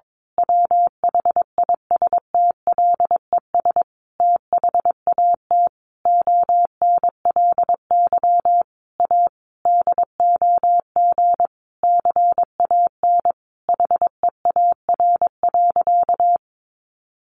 Morse code, heard as a keyed tone, about 22 wpm, 700 Hz. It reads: WHISTLES THAT ONLY A DOG CAN HEAR.
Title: A Sound Of Thunder 22wpm